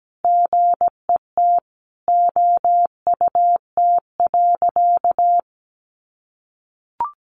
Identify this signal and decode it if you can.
Morse code, heard as a keyed tone, about 17 wpm, 700 Hz. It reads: GET OUT.
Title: A Sound Of Thunder 17wpm